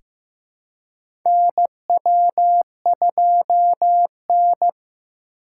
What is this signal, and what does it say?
Morse code, heard as a keyed tone, about 15 wpm, 700 Hz. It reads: NW2N
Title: USA Call Signs - 4 Characters - 4x Character Spacing - 15wpm